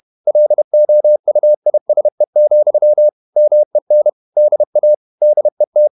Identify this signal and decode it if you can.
Morse code, heard as a keyed tone, about 31 wpm, 600 Hz. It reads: LOUISE, MEN DA DET
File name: Ferie QRQ 31wpm